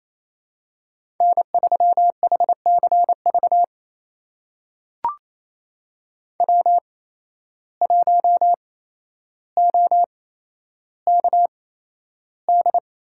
Morse code, heard as a keyed tone, about 28 wpm, spaced as for 12 wpm, 700 Hz. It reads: N3HCV W1OKD
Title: USA Call Signs - 5 Characters - 8x Character Spacing - 28wpm